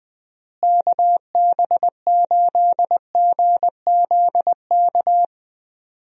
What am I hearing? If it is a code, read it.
Morse code, heard as a keyed tone, about 21 wpm, 700 Hz. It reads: KB8GZK
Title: USA Call Signs - 6 Characters - Repeated 2x - 20wpm